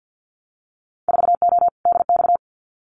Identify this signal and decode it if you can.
Morse code, heard as a keyed tone, about 50 wpm, 700 Hz. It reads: 4O DX
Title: Winter Field Day Exchanges Encore 50wpm